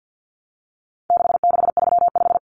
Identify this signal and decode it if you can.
Morse code, heard as a keyed tone, about 50 wpm, 700 Hz. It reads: 6635
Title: Numbers - 4-Digit 50wpm